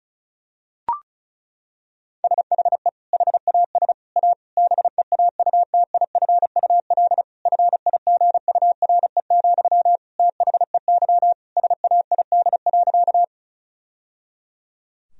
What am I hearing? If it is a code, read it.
Morse code, heard as a keyed tone, about 34 wpm, 700 Hz. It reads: SHE HAS A BEAUTIFUL FIGURE, THEY SAID.
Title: Sentences from Top 1000 Words - Review Part 2 - 35wpm